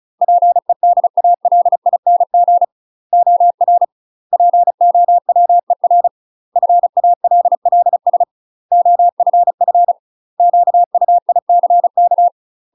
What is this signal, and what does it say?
Morse code, heard as a keyed tone, about 36 wpm, 700 Hz. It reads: PEDALING OR POWER FALLS OFF QUICK
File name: UK MASTER QSOs TRAINING / 35WPM / UK MASTER QSOs 0017